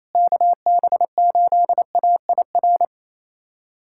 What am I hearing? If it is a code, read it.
Morse code, heard as a keyed tone, about 28 wpm, 700 Hz. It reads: KB8AIR
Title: USA Callsign DE Callsign 28wpm